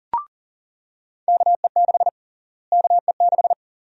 Morse code, heard as a keyed tone, about 40 wpm, 700 Hz. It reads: KE6 KE6
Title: USA Call Sign Prefixes - Repeated 2x - 40wpm